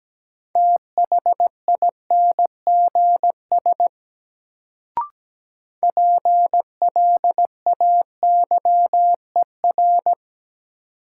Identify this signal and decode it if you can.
Morse code, heard as a keyed tone, about 17 wpm, 700 Hz. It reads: THINGS PLAYER
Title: Sentences from Top 1000 Words - Review Part 4 - New Word Drill - 17wpm